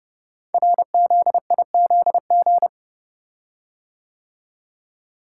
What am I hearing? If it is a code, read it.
Morse code, heard as a keyed tone, about 29 wpm, 700 Hz. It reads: RZIZG
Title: Code Groups - 5 Letters - 30wpm